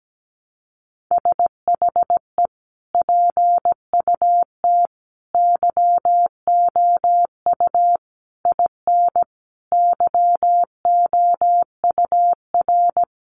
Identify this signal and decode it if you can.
Morse code, heard as a keyed tone, about 17 wpm, 700 Hz. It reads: SHE PUT YOU IN YOUR
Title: Sentences from Top 800 Words 17wpm